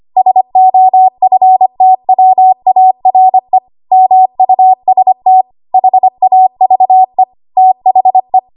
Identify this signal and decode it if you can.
Morse code, heard as a keyed tone, about 25 wpm, 750 Hz. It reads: SOFTWARE MUST HAVE THE
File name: Archive / 25 WPM / 25WPM